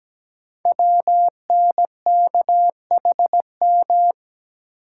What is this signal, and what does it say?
Morse code, heard as a keyed tone, about 17 wpm, 700 Hz. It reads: WNKHM
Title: Code Groups - 5 Letters - 17wpm